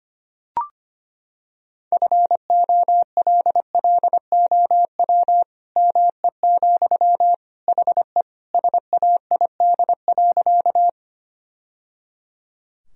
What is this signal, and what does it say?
Morse code, heard as a keyed tone, about 25 wpm, 700 Hz. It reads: FOLLOW ME, HE SAID.
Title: Sentences from Top 1000 Words - Review Part 1 - 25wpm